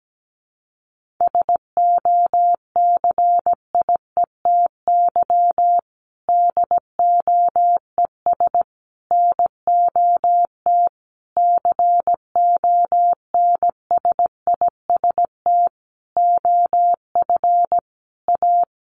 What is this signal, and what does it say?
Morse code, heard as a keyed tone, about 17 wpm, 700 Hz. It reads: SOCIETY DOES NOT CONSIST OF A
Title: Sentences from Top 1400 Words 17wpm